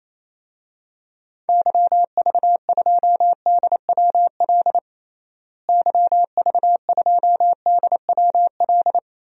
Morse code, heard as a keyed tone, about 28 wpm, 700 Hz. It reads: YV2DWL YV2DWL
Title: DXCC-100 Call Signs - 6 Characters - Repeated 3x - 28wpm